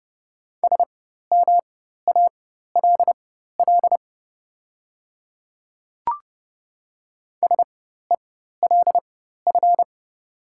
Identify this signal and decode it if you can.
Morse code, heard as a keyed tone, about 30 wpm, spaced as for 14 wpm, 700 Hz. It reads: SMALL SELF
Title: 4x Character Spacing - Top 200 Words - 30wpm